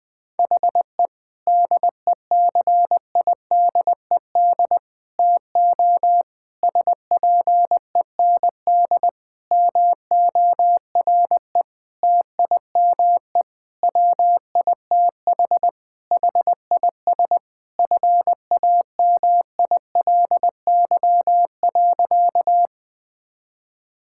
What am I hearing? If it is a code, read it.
Morse code, heard as a keyed tone, about 20 wpm, 700 Hz. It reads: HE DECIDED TO SPEND MORE TIME WITH HIS FAMILY.
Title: Sentences with 7-Word Phrases - Spoken Beginning - 20wpm